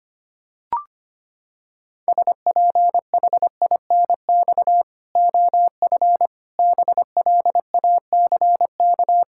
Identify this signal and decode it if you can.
Morse code, heard as a keyed tone, about 25 wpm, 700 Hz. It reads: SPHINX OF BLACK
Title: Pangram Challenge v2 - 25wpm